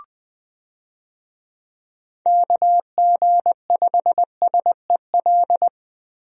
Morse code, heard as a keyed tone, about 20 wpm, 700 Hz. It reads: KG5SEL
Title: USA Call Signs - 6 Characters - Repeated 3x - 20wpm